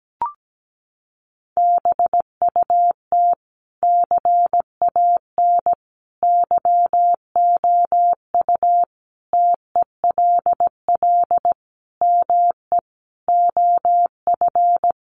Morse code, heard as a keyed tone, about 17 wpm, 700 Hz. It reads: BUT CAN YOU TELL ME OF
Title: Sentences from Top 1400 Words 17wpm